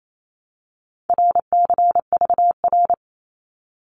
Morse code, heard as a keyed tone, about 28 wpm, 700 Hz. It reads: RCVR
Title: Abbreviations 28wpm